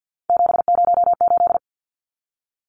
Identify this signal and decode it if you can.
Morse code, heard as a keyed tone, about 50 wpm, 700 Hz. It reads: 708
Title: Numbers - 3-Digit 50wpm